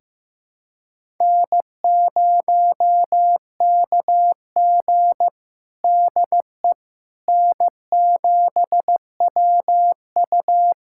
Morse code, heard as a keyed tone, about 16 wpm, 700 Hz. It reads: N0KG DE N7WU
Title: USA Callsign DE Callsign - 4 Characters - 15wpm